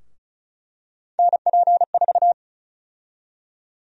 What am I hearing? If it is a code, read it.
Morse code, heard as a keyed tone, about 34 wpm, 700 Hz. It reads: NP4
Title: USA Call Sign Prefixes - Repeated 2x - 35wpm